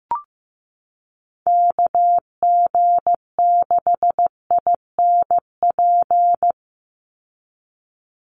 Morse code, heard as a keyed tone, about 15 wpm, 700 Hz. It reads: KG6INP
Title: USA Call Signs 15wpm